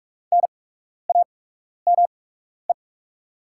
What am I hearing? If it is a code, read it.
Morse code, heard as a keyed tone, about 44 wpm, spaced as for 12 wpm, 700 Hz. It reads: NAME
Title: 8x Character Spacing - Top 500 Words - 45wpm